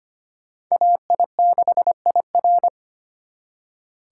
Morse code, heard as a keyed tone, about 32 wpm, 700 Hz. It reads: AI6IR